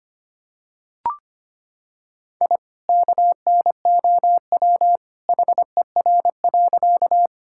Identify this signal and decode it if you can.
Morse code, heard as a keyed tone, about 25 wpm, 700 Hz. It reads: I KNOW HER.
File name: practice / 25wpm / Sentences with 2-Word Phrases - 25wpm